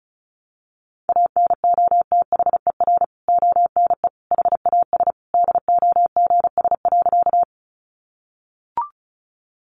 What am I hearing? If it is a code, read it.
Morse code, heard as a keyed tone, about 34 wpm, 700 Hz. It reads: ANOTHER ONE HAS DOGS.
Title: Additional Easy Phrases 35wpm